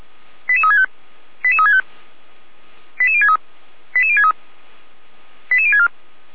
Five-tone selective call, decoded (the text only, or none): 88365 88365 88063 88063 88064